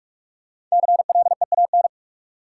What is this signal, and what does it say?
Morse code, heard as a keyed tone, about 46 wpm, 700 Hz. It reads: CLEAN
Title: Sentences from Top 1100 Words - New Word Drill Alt - 45wpm